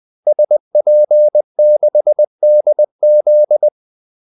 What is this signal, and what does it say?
Morse code, heard as a keyed tone, about 20 wpm, 600 Hz. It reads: SP6DZ
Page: Kaldesignaler 20 wpm